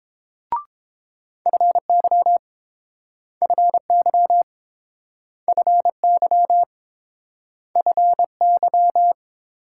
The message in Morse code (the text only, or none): FY FY FY FY